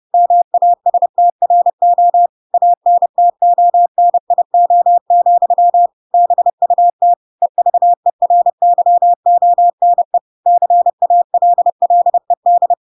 Morse code, heard as a keyed tone, about 30 wpm, 700 Hz. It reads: MASTRO ANTONIO, BUT EVERYONE CALLED
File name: Pinocchio / Pinocchio ch1 30 30 ik1hge